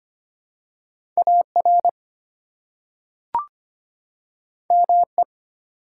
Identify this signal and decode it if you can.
Morse code, heard as a keyed tone, about 25 wpm, 700 Hz. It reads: AR ME
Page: morse-code-ninja-practice-25wpm